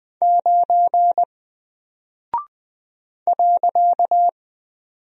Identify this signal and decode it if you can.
Morse code, heard as a keyed tone, about 20 wpm, 700 Hz. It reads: 9 .
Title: Single Character (Letter:Number:Punctuation:Prosign) 20wpm